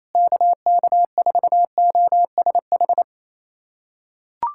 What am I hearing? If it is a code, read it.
Morse code, heard as a keyed tone, about 27 wpm, 700 Hz. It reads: KK4OSH
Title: USA Callsign DE Callsign 28wpm